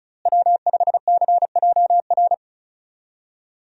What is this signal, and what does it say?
Morse code, heard as a keyed tone, about 34 wpm, 700 Hz. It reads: W5CJR
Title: USA Call Signs - 5 Characters - Repeated 3x - 35wpm